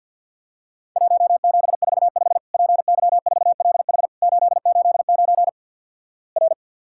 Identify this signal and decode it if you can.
Morse code, heard as a keyed tone, about 47 wpm, 700 Hz. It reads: 1745 PY4LH 889 R
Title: Worked All Europe DX Contest - QTC Exchange - 50wpm